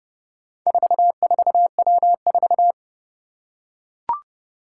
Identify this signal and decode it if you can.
Morse code, heard as a keyed tone, about 30 wpm, 700 Hz. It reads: 44W4
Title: Four Challenging Letter-Number Combinations 30wpm